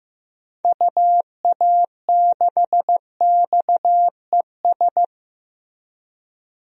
Morse code, heard as a keyed tone, about 15 wpm, 700 Hz. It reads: UA6XES
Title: DXCC-100 Call Signs - 6 Characters - Repeated 2x - 15wpm